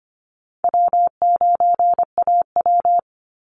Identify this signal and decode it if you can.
Morse code, heard as a keyed tone, about 25 wpm, 700 Hz. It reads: W9AW
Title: Easy Call Signs - 25wpm